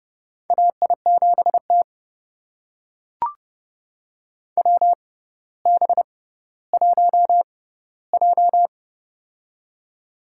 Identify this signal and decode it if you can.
Morse code, heard as a keyed tone, about 30 wpm, spaced as for 15 wpm, 700 Hz. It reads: AI7T WB1J
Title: USA Call Signs - 4 Characters - 6x Character Spacing - 30wpm